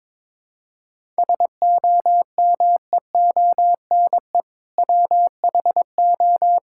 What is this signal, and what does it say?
Morse code, heard as a keyed tone, about 21 wpm, 700 Hz. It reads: SOMEONE WHO
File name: Sentences with 7-Word Phrases - 22wpm